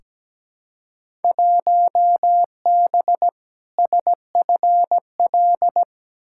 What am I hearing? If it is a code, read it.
Morse code, heard as a keyed tone, about 17 wpm, 700 Hz. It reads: 1B SFL
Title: Field Day Exchanges Encore - 17wpm